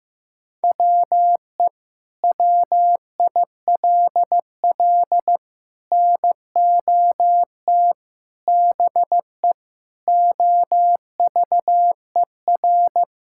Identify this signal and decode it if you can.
Morse code, heard as a keyed tone, about 16 wpm, 700 Hz. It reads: WE WILL NOT BE OVER
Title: Sentences from Top 1000 Words - Review Part 5 - 15wpm